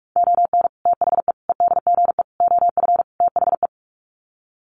Morse code, heard as a keyed tone, about 44 wpm, 700 Hz. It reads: ON THE EDGE OF THE
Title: Sets of 5 Words 45wpm